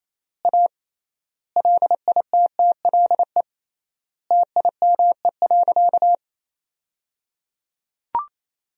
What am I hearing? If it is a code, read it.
Morse code, heard as a keyed tone, about 28 wpm, 700 Hz. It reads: A LITTLE TIME.
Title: 3x Word-Spacing - Sentences from Top 200 Words - 28wpm